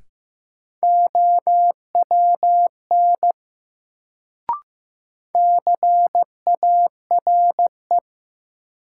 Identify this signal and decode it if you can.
Morse code, heard as a keyed tone, about 15 wpm, 700 Hz. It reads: OWN CARE
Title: Sentences from Top 300 Words - New Word Drill - 15wpm